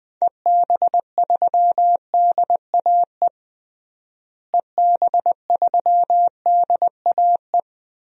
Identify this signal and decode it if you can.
Morse code, heard as a keyed tone, about 20 wpm, 700 Hz. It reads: EB3DAE EB3DAE
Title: DXCC-100 Call Signs - 6 Characters - Repeated 3x - 20wpm